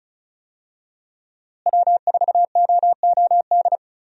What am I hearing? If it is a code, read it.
Morse code, heard as a keyed tone, about 34 wpm, 700 Hz. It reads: W4OOD
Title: USA Call Signs - 5 Characters - Repeated 2x - 35wpm